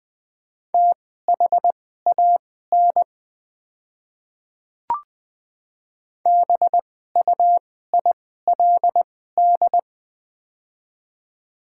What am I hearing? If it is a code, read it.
Morse code, heard as a keyed tone, about 20 wpm, spaced as for 14 wpm, 700 Hz. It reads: THAN BUILD
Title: 2x Character Spacing - Top 200 Words - 20wpm